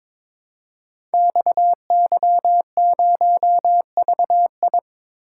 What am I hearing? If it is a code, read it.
Morse code, heard as a keyed tone, about 22 wpm, 700 Hz. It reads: XY0VI
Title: Code Groups - 5 Letter:Numbers - 22wpm